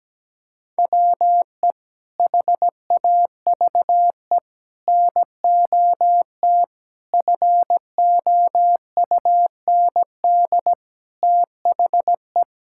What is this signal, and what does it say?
Morse code, heard as a keyed tone, about 17 wpm, 700 Hz. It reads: WE HAVE NOT FOUND THE
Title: Sentences from Top 1000 Words - Review Part 6 - 17wpm